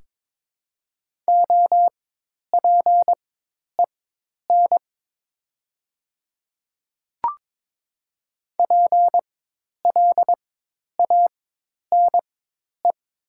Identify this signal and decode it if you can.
Morse code, heard as a keyed tone, about 22 wpm, spaced as for 10 wpm, 700 Hz. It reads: OPEN PLANE
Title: 4x Character Spacing - Top 500 Words - 22wpm